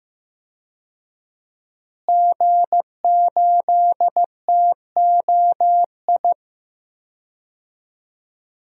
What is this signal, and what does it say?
Morse code, heard as a keyed tone, about 15 wpm, 700 Hz. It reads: G8TOI
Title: DXCC-100 Call Signs - 5 Characters - Repeated 3x - 15wpm